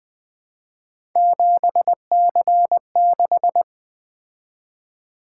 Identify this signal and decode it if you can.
Morse code, heard as a keyed tone, about 20 wpm, 700 Hz. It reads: 7C6
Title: Three Letter-Number Combinations - 20wpm